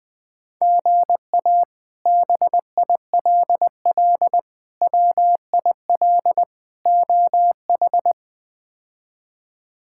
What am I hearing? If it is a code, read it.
Morse code, heard as a keyed tone, about 20 wpm, 700 Hz. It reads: GA BILL WIL OH
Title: K1USN SST QSO Practice - 20wpm